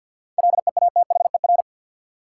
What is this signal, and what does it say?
Morse code, heard as a keyed tone, about 50 wpm, 700 Hz. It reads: LEATHER
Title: Seven Letter Words - 50wpm